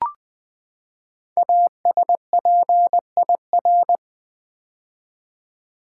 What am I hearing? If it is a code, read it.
Morse code, heard as a keyed tone, about 20 wpm, 700 Hz. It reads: ASPIR
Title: Build A Word - 5 Letters At A Time - 10-Letter Words - 20wpm